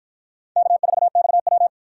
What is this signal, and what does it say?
Morse code, heard as a keyed tone, about 47 wpm, 700 Hz. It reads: K4XW